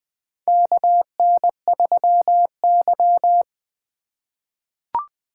Morse code, heard as a keyed tone, about 20 wpm, 700 Hz. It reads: KN3Y
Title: 2023 SOTA Chaser Call Signs - 20wpm